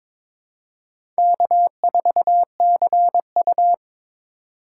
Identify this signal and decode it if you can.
Morse code, heard as a keyed tone, about 22 wpm, 700 Hz. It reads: K4CU
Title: USA Call Signs - 4 Characters - 2x Character Spacing - 22wpm